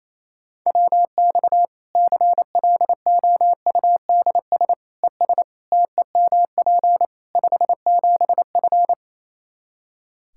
Morse code, heard as a keyed tone, about 28 wpm, 700 Hz. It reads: WX CLOUDS ES TEMP 57F
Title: Practice QSO - Exchange 2 - Info - 28wpm